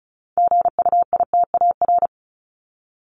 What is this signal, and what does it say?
Morse code, heard as a keyed tone, about 34 wpm, 700 Hz. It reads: GUITAR
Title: Top 1501-1750 Words 35wpm